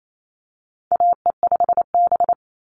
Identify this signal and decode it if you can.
Morse code, heard as a keyed tone, about 28 wpm, 700 Hz. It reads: AE5B
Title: NA SOTA Chaser Callsigns 28wpm